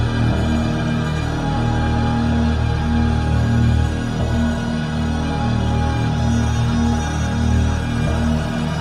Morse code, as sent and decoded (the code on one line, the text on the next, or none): none